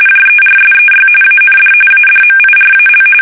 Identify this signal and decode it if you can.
RTTY, about 45.45 baud. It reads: JERSEY AND ADMIRAL
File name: fsk850 / battleshipnj